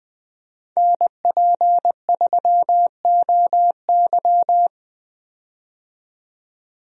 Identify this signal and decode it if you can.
Morse code, heard as a keyed tone, about 20 wpm, 700 Hz. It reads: NP3OY
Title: USA Call Signs - 5 Characters - 20wpm